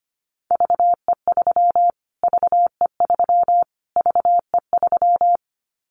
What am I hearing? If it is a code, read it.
Morse code, heard as a keyed tone, about 25 wpm, 700 Hz. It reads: VE3 VE3 VE3
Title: Callsign Prefixes for Top 25 DXCC - Repeated 3x - 25wpm